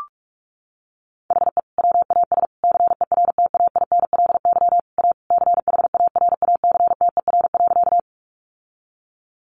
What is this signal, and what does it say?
Morse code, heard as a keyed tone, about 44 wpm, 700 Hz. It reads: HE WAS CERTAINLY A CHARACTER.
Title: Sentences from Top 1200 Words 45wpm